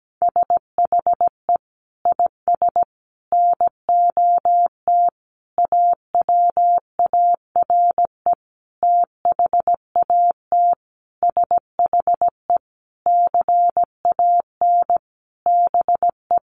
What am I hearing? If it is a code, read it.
Morse code, heard as a keyed tone, about 17 wpm, 700 Hz. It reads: SHE IS NOT AWARE THAT SHE CAN BE
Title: Sentences from Top 1200 Words 17wpm